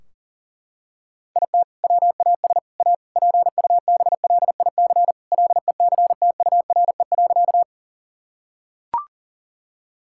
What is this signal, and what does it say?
Morse code, heard as a keyed tone, about 40 wpm, 700 Hz. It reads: IT WAS A PUBLIC LECTURE.
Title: Sentences from Top 1600 Words 40wpm